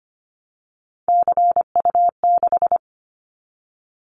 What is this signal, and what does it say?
Morse code, heard as a keyed tone, about 25 wpm, 700 Hz. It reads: CU6
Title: Callsign Prefixes for Top 50 DXCC - 25wpm